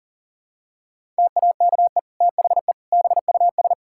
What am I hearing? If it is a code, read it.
Morse code, heard as a keyed tone, about 37 wpm, 700 Hz. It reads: TAKE THE BUS
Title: Sentences from Top 1000 Words - Review Part 6 - 40wpm